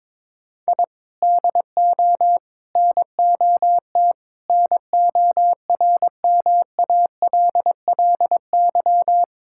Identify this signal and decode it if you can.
Morse code, heard as a keyed tone, about 22 wpm, 700 Hz. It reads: I DO NOT NORMALLY
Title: Sentences from Top 1000 Words - Review Part 6 - 22wpm